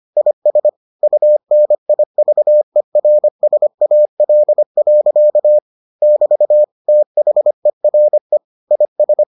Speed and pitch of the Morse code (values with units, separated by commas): 25 wpm, 600 Hz